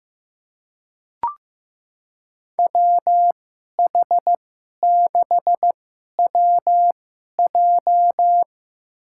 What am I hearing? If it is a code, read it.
Morse code, heard as a keyed tone, about 15 wpm, spaced as for 11 wpm, 700 Hz. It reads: WH6WJ
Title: USA Call Signs - 5 Characters - 2x Character Spacing - 15wpm